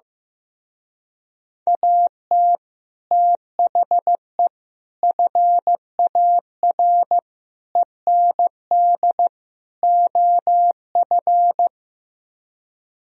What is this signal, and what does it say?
Morse code, heard as a keyed tone, about 15 wpm, 700 Hz. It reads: AT THE FAR END OF
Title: Speed-Racing - Sets of 5 Words 15wpm